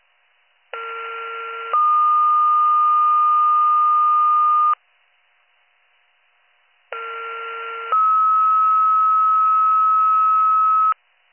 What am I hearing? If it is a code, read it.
Two-tone page, heard as a one-second tone, then a three-second tone, about 1370 then 1220 Hz, about 1370 then 1290 Hz. It is heard from 0.7 to 10.9 s.